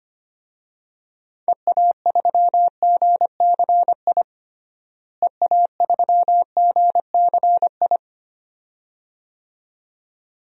Morse code, heard as a keyed tone, about 25 wpm, 700 Hz. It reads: EA3GCI EA3GCI
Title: DXCC-100 Call Signs - 6 Characters - Repeated 2x - 25wpm